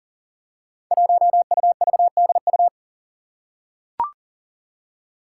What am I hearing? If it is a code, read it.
Morse code, heard as a keyed tone, about 40 wpm, 700 Hz. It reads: 1UVDU
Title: Five Challenging Letter-Number Combinations 40wpm